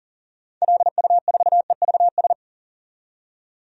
Morse code, heard as a keyed tone, about 40 wpm, 700 Hz. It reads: LU4EVS
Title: DXCC-100 Call Signs - 6 Characters - 40wpm